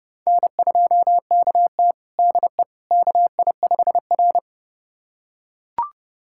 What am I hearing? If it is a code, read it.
Morse code, heard as a keyed tone, about 30 wpm, 700 Hz. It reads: N2KT DE KI5R
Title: USA Callsign DE Callsign - 4 Characters - 30wpm